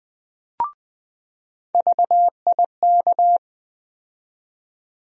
Morse code, heard as a keyed tone, about 20 wpm, 700 Hz. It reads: VIK
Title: USA Call Sign Postfixes - 3 Letters 20wpm